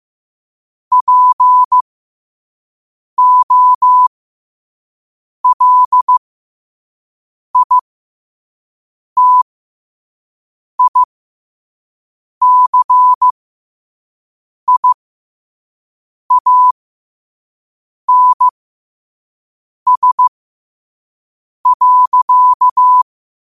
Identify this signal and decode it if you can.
Morse code, heard as a keyed tone, about 15 wpm, spaced as for 5 wpm, 1000 Hz. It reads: POLITICIANS.
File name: 2026-02-15-5wpm